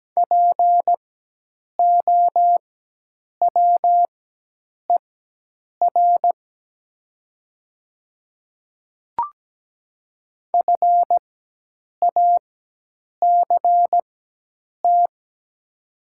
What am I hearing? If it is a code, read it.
Morse code, heard as a keyed tone, about 17 wpm, spaced as for 8 wpm, 700 Hz. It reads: POWER FACT
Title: 4x Character Spacing - Top 500 Words - 17wpm